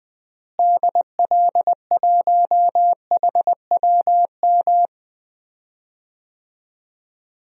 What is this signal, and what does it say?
Morse code, heard as a keyed tone, about 20 wpm, 700 Hz. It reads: DL1HWM
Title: DXCC-100 Call Signs - 6 Characters - Repeated 3x - 20wpm